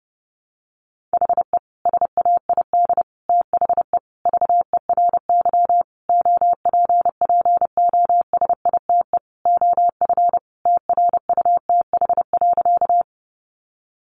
Morse code, heard as a keyed tone, about 30 wpm, 700 Hz. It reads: HE SAID THE VERY OPPOSITE OF TRUTH.
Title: Sentences from Top 1300 Words 30wpm